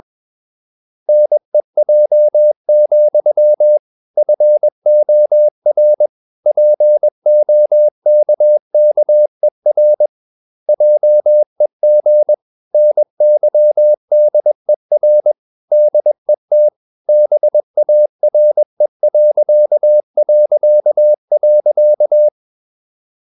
Morse code, heard as a keyed tone, about 21 wpm, 600 Hz. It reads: NEJ, FOR POKKER JEG NYDER DET BARE...
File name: Ferie QRS 21wpm